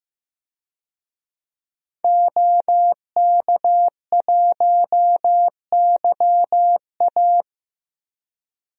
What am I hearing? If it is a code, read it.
Morse code, heard as a keyed tone, about 15 wpm, 700 Hz. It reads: OK1YA
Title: DXCC-100 Call Signs - 5 Characters - Repeated 2x - 15wpm